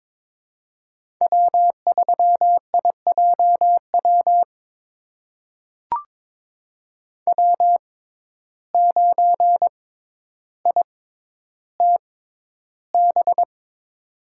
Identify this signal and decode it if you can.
Morse code, heard as a keyed tone, about 22 wpm, spaced as for 11 wpm, 700 Hz. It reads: W3IJW W9ITB
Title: USA Call Signs - 5 Characters - 6x Character Spacing - 22wpm